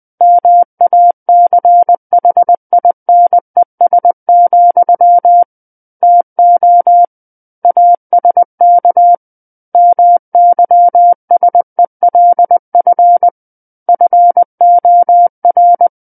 Morse code, heard as a keyed tone, about 20 wpm, 700 Hz. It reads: MACHINES, TO ASK MYSELF FOR
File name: 20 WPM